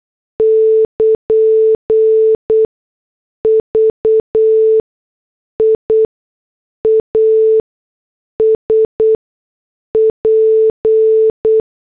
Morse code, heard as a keyed tone, about 8 wpm, spaced as for 6 wpm, 440 Hz. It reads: <KN> VIA SP
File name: morse / headlines